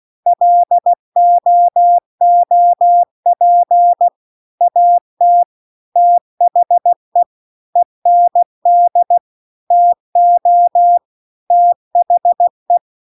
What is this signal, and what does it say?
Morse code, heard as a keyed tone, about 16 wpm, 700 Hz. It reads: LOOP AT THE END TO THE
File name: CW-The-Big-Twoheaded-River-2-16WPM